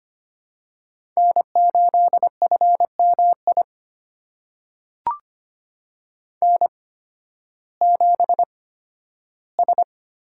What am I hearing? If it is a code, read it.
Morse code, heard as a keyed tone, about 25 wpm, spaced as for 13 wpm, 700 Hz. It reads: N8FMI N7S
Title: USA Call Signs - 5 Characters - 8x Character Spacing - 25wpm